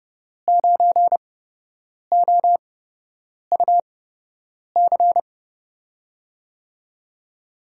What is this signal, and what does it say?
Morse code, heard as a keyed tone, about 30 wpm, spaced as for 8 wpm, 700 Hz. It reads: 9OUC
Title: USA Call Signs - 5 Characters - 8x Character Spacing - 30wpm